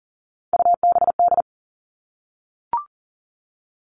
Morse code, heard as a keyed tone, about 40 wpm, 700 Hz. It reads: UBD
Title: All Cut Numbers - 3-Digit 40wpm